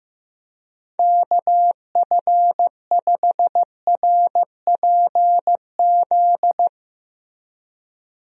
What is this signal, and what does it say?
Morse code, heard as a keyed tone, about 16 wpm, 700 Hz. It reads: KF5RPZ
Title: USA Call Signs - 6 Characters - Repeated 2x - 15wpm